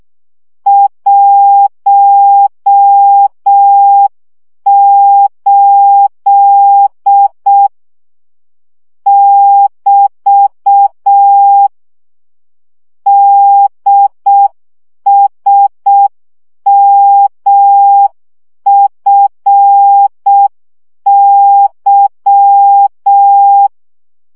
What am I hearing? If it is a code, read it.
Morse code, heard as a keyed tone, about 6 wpm, 800 Hz. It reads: 18 <BT> DSMFY
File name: morze vizsga 2011-1 5